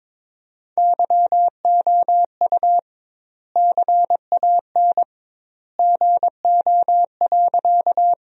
Morse code, heard as a keyed tone, about 22 wpm, 700 Hz. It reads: YOU CAN GO.
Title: 2x Word-Spacing - Sentences from Top 100 Words - 22wpm